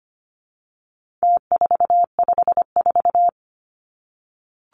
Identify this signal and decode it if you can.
Morse code, heard as a keyed tone, about 25 wpm, 700 Hz. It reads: T454
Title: Numbers - Contest Style - 4-Digit 25wpm